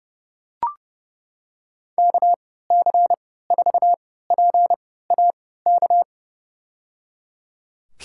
Morse code, heard as a keyed tone, about 30 wpm, spaced as for 17 wpm, 700 Hz. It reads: KC4PAK
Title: USA Call Signs - 6 Characters - 3x Character Spacing - 30wpm